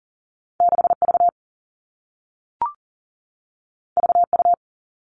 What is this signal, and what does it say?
Morse code, heard as a keyed tone, about 42 wpm, 700 Hz. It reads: 6V VU
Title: All Cut Numbers - 2-Digit 40wpm